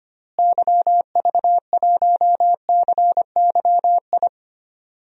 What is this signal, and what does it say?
Morse code, heard as a keyed tone, about 25 wpm, 700 Hz. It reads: YV1CYI
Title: DXCC-100 Call Signs - 6 Characters - Repeated 2x - 25wpm